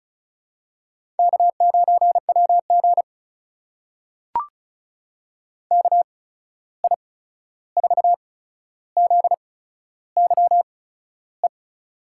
Morse code, heard as a keyed tone, about 36 wpm, spaced as for 13 wpm, 700 Hz. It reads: K9WG KI4ZYE
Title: USA Call Signs - 8x Character Spacing - 35wpm